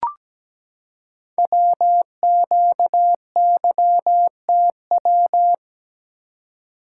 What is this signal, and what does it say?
Morse code, heard as a keyed tone, about 17 wpm, 700 Hz. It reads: WQYTW
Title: Code Groups - 5 Letters - 17wpm